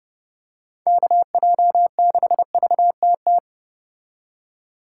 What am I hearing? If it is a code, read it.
Morse code, heard as a keyed tone, about 29 wpm, 700 Hz. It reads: KJ6VTT